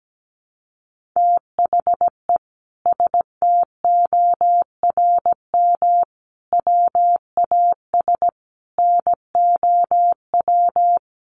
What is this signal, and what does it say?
Morse code, heard as a keyed tone, about 17 wpm, 700 Hz. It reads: THE STORM WAS NOW
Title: Sentences from Top 1500 Words 17wpm